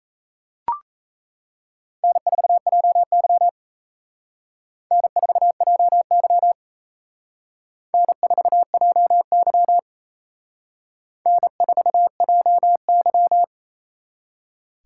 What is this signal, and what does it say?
Morse code, heard as a keyed tone, about 35 wpm, 700 Hz. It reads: N4JY N4JY N4JY N4JY
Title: Speed Racing - USA Call Signs 28wpm